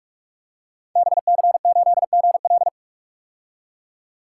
Morse code, heard as a keyed tone, about 44 wpm, 700 Hz. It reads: BC8GL
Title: Code Groups - 5 Letter:Numbers - 45wpm